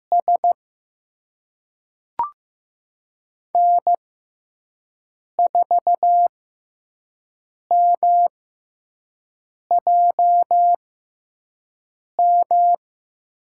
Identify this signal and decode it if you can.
Morse code, heard as a keyed tone, about 15 wpm, spaced as for 5 wpm, 700 Hz. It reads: S N4MJM
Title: USA Call Signs - 6x Character Spacing - 15wpm